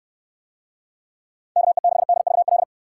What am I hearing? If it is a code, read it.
Morse code, heard as a keyed tone, about 63 wpm, 700 Hz. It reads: KE6DVL